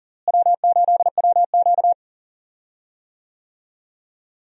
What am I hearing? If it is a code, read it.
Morse code, heard as a keyed tone, about 40 wpm, 700 Hz. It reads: W8WQ
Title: USA Call Signs - 4 Characters - Repeated 2x - 40wpm